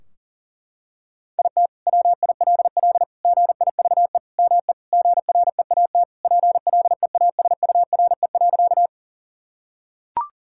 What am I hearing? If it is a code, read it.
Morse code, heard as a keyed tone, about 40 wpm, 700 Hz. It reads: IT WILL GIVE ME GREAT PLEASURE.
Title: Sentences from Top 1600 Words 40wpm